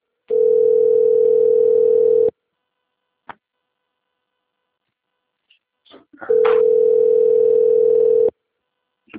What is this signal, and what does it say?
Ringback tone (North America), 440 + 480 Hz.
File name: rg-1600-unknown- - - .9261